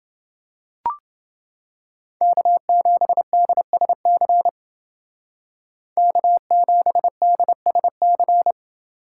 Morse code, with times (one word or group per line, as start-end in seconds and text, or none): 2.2-4.5 K7DSC
6.0-8.5 K7DSC